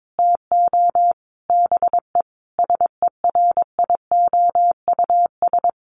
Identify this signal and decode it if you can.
Morse code, heard as a keyed tone, about 22 wpm, 700 Hz. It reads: TO BE SERIOUS
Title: Sentences from Top 1100 Words 22wpm